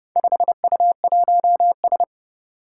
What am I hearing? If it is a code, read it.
Morse code, heard as a keyed tone, about 30 wpm, 700 Hz. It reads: 5U1S
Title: Four Challenging Letter-Number Combinations 30wpm